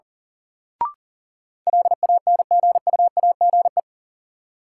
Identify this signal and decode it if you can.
Morse code, heard as a keyed tone, about 37 wpm, 700 Hz. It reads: LANGUAGE
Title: Sentences from Top 500 Words - New Word Drill - 40wpm